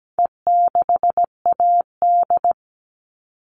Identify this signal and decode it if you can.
Morse code, heard as a keyed tone, about 17 wpm, 700 Hz. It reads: E6AD
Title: All Cut Numbers - 4-Digit 17wpm